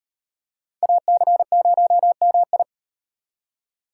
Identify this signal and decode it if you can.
Morse code, heard as a keyed tone, about 38 wpm, 700 Hz. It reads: AC0MI